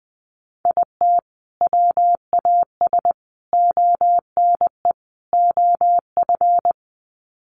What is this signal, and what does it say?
Morse code, heard as a keyed tone, about 20 wpm, 700 Hz. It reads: IT WAS ONE OF
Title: Sets of 4 Words 20wpm